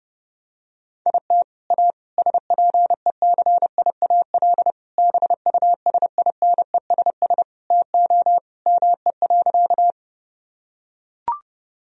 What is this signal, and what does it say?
Morse code, heard as a keyed tone, about 30 wpm, 700 Hz. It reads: IT A SPECIAL BUSINESS TO ME.
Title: Sentences from Top 1000 Words - Review Part 3 - 30wpm